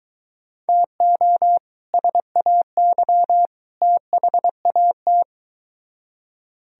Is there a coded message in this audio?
Yes: Morse code from 0.7 to 5.2 s, heard as a keyed tone, about 23 wpm, 700 Hz. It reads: TO SAY THAT